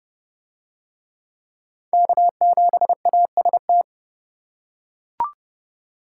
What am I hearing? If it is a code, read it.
Morse code, heard as a keyed tone, about 30 wpm, 700 Hz. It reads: K7AST